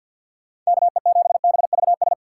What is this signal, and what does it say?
Morse code, heard as a keyed tone, about 62 wpm, 700 Hz. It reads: KE7BVS
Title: Speed Racing - USA Call Signs 50wpm